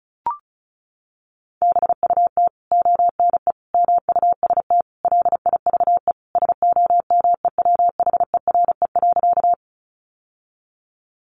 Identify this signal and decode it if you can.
Morse code, heard as a keyed tone, about 34 wpm, 700 Hz. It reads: BUT ONE MUST LIVE SOMEWHERE.
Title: Sentences from Top 1400 Words 35wpm